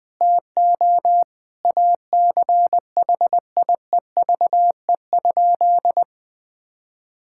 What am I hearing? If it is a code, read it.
Morse code, heard as a keyed tone, about 20 wpm, 700 Hz. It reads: TO ACHIEVE?
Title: Sentences from Top 1000 Words - Review Part 6 - 20wpm